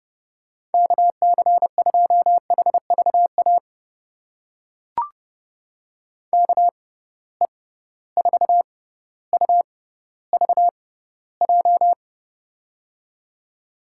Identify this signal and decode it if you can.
Morse code, heard as a keyed tone, about 30 wpm, spaced as for 15 wpm, 700 Hz. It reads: KC2HVA KE4UVJ
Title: USA Call Signs - 6 Characters - 6x Character Spacing - 30wpm